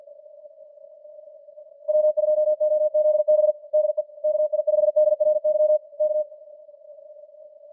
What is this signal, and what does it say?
Morse code, heard as a keyed tone, about 48 wpm, 610 Hz. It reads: W2OZB DE KI5DRY K